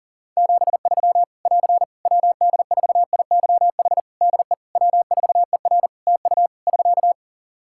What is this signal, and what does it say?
Morse code, heard as a keyed tone, about 40 wpm, 700 Hz. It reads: 73 <AR> WD4IYH DE W4ER TU <SK>
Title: Practice QSO - Exchange 4 - Ending - 40wpm